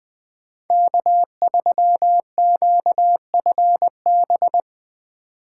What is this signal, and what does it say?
Morse code, heard as a keyed tone, about 20 wpm, 700 Hz. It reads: K3QFB
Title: USA Call Signs - 2x Character Spacing - 20wpm